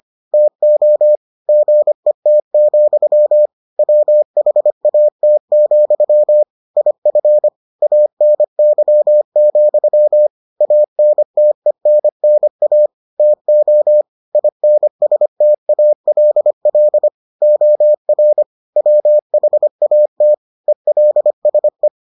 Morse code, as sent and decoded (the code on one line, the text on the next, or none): - --- / --. - --..-- / .-- .... .- - --..-- / .. ..-. / .- -. -.-- --..-- / .- -. - . -. -. .- / - --- / .. -. ... - .- .-.. .-.. / --- .-. / .-- .... .- - / . .-.. ... .
TO GET, WHAT, IF ANY, ANTENNA TO INSTALL OR WHAT ELSE